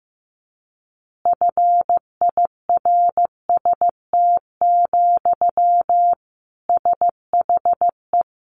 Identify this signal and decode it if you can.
Morse code, heard as a keyed tone, about 15 wpm, 700 Hz. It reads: FIRST, SHE
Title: Easy Phrases 15wpm